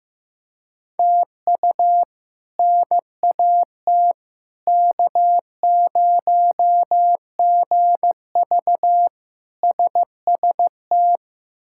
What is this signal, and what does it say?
Morse code, heard as a keyed tone, about 15 wpm, 700 Hz. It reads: TU NAT K0GV SST
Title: K1USN SST QSO Practice - 15wpm